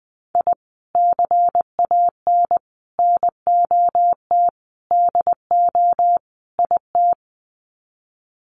Morse code, heard as a keyed tone, about 20 wpm, 700 Hz. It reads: I CAN NOT DO IT
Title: Sets of 5 Words Encore 20wpm